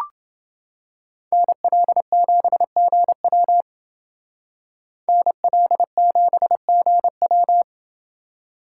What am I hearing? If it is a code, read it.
Morse code, heard as a keyed tone, about 28 wpm, 700 Hz. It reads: NL7GW NL7GW